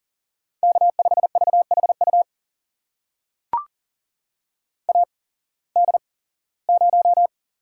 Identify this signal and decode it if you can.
Morse code, heard as a keyed tone, about 40 wpm, spaced as for 21 wpm, 700 Hz. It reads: K5VHU AD0
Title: USA Call Signs - 5 Characters - 8x Character Spacing - 40wpm